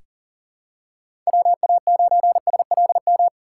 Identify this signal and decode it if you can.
Morse code, heard as a keyed tone, about 39 wpm, 700 Hz. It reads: WA9SLM
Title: USA Call Signs - 6 Characters - 4x Character Spacing - 40wpm